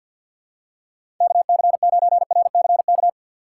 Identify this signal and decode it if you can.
Morse code, heard as a keyed tone, about 48 wpm, 700 Hz. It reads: KC9RCK
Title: USA Call Signs - 6 Characters - 3x Character Spacing - 50wpm